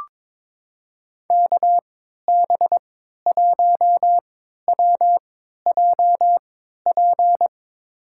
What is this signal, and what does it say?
Morse code, heard as a keyed tone, about 22 wpm, spaced as for 12 wpm, 700 Hz. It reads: KB1WJP
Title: USA Call Signs - 6 Characters - 3x Character Spacing - 22wpm